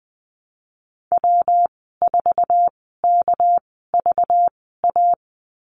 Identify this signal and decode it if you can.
Morse code, heard as a keyed tone, about 20 wpm, spaced as for 15 wpm, 700 Hz. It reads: W4KVA
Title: 238 Lesson 38g - Letter q - Call Signs 20wpm